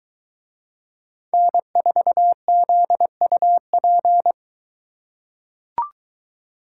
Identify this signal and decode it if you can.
Morse code, heard as a keyed tone, about 22 wpm, 700 Hz. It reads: N4ZUP